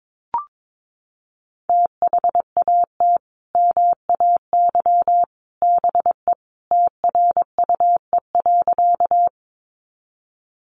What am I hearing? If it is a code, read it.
Morse code, heard as a keyed tone, about 22 wpm, 700 Hz. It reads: THAT MAY BE TRUE.
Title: Sentences from Top 400 Words 22wpm